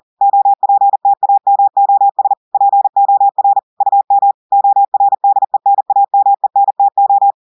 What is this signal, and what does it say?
Morse code, heard as a keyed tone, about 40 wpm, 800 Hz. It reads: OPTAMOS POR UM ORDENAMENTO
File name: audios / portugais / portugais texte en clair 1 vitesse 40 mots minutes